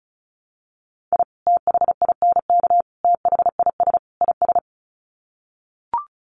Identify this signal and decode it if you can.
Morse code, heard as a keyed tone, about 34 wpm, 700 Hz. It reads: I THINK THIS IS
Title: Sets of 4 Words 35wpm